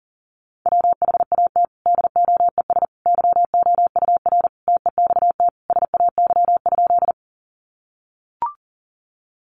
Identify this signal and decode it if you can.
Morse code, heard as a keyed tone, about 40 wpm, 700 Hz. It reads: WHAT DOES YOUR TEXT SAY?
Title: Sentences from Top 1200 Words 40wpm